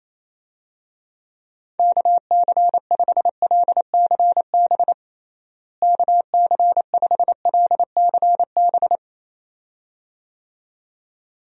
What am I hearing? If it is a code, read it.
Morse code, heard as a keyed tone, about 28 wpm, 700 Hz. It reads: KC5LCB KC5LCB
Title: USA Call Signs - 6 Characters - Repeated 2x - 28wpm